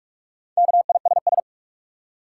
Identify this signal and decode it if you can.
Morse code, heard as a keyed tone, about 43 wpm, 700 Hz. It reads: KISS
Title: Four Letter Words - 45wpm